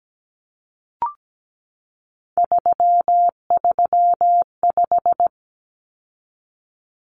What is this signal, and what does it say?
Morse code, heard as a keyed tone, about 17 wpm, 700 Hz. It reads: 335
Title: Numbers - 3-Digit 17wpm